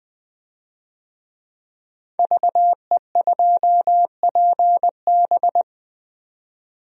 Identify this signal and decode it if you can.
Morse code, heard as a keyed tone, about 20 wpm, 700 Hz. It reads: VE2PB
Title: DXCC-100 Call Signs - 5 Characters - 20wpm